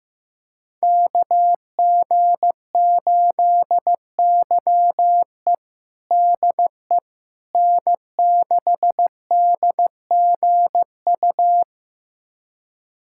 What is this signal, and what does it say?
Morse code, heard as a keyed tone, about 16 wpm, 700 Hz. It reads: KG8YE DE N6DGU
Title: USA Callsign DE Callsign - 5 Characters - 15wpm